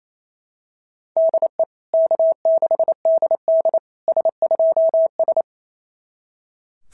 Morse code, heard as a keyed tone, about 28 wpm, 690 Hz. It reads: DE K6DD S2S
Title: SOTA QSO Practice 28wpm